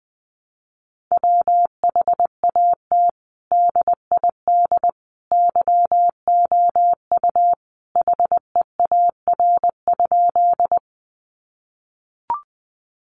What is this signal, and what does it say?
Morse code, heard as a keyed tone, about 20 wpm, 700 Hz. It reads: WHAT DID YOU HEAR?
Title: Sentences from Top 300 Words 20wpm